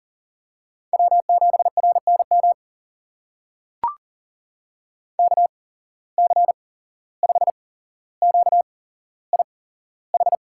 Morse code, heard as a keyed tone, about 40 wpm, spaced as for 16 wpm, 700 Hz. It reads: W7RNM KC5QIH
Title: USA Call Signs - 8x Character Spacing - 40wpm